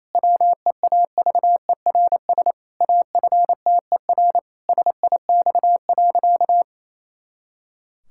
Morse code, heard as a keyed tone, about 28 wpm, 700 Hz. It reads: WEAVERS AFTER SIX.
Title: Pangram Challenge v2 - 28wpm